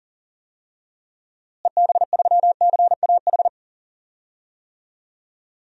Morse code, heard as a keyed tone, about 40 wpm, 700 Hz. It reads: EB3CAH
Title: DXCC-100 Call Signs - 6 Characters - Repeated 2x - 40wpm